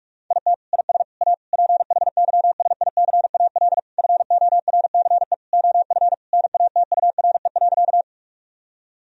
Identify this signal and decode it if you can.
Morse code, heard as a keyed tone, about 47 wpm, 700 Hz. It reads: IT IS A PHYSICAL FORCE OF NATURE.
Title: Sentences from Top 1000 Words - Review Part 4 - 45wpm